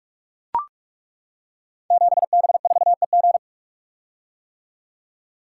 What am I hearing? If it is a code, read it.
Morse code, heard as a keyed tone, about 42 wpm, 700 Hz. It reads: 7B4EG
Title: Code Groups - 5 Letter:Numbers - 45wpm